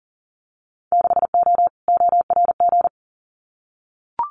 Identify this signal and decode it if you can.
Morse code, heard as a keyed tone, about 40 wpm, 700 Hz. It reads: 6O ORG
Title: Winter Field Day Exchanges Encore 40wpm